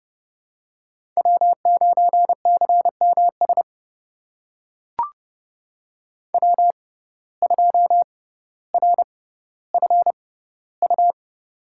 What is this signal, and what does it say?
Morse code, heard as a keyed tone, about 30 wpm, spaced as for 15 wpm, 700 Hz. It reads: W9CMS W2RFU
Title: USA Call Signs - 5 Characters - 6x Character Spacing - 30wpm